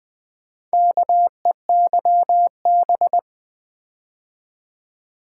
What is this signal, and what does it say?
Morse code, heard as a keyed tone, about 20 wpm, 700 Hz. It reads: KEYB
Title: Build A Word - 2 Letters At A Time - 8-Letter Words - 20wpm